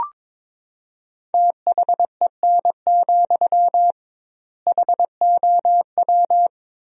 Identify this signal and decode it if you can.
Morse code, heard as a keyed tone, about 22 wpm, 700 Hz. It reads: THEN, HOW
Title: 2x Word-Spacing - Sentences from Top 100 Words - 22wpm